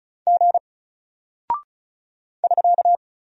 Morse code, heard as a keyed tone, about 36 wpm, 700 Hz. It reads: G <SK>
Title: Single Character (Letter:Number:Punctuation:Prosign) Alt 35wpm